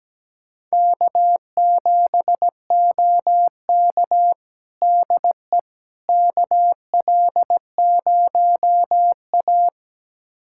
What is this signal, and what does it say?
Morse code, heard as a keyed tone, about 17 wpm, 700 Hz. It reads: K7OK DE KL0A
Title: USA Callsign DE Callsign - 4 Characters - 17wpm